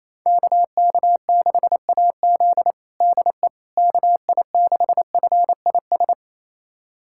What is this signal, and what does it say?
Morse code, heard as a keyed tone, about 27 wpm, 700 Hz. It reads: KK6AZ DE KI6FIS
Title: USA Callsign DE Callsign 28wpm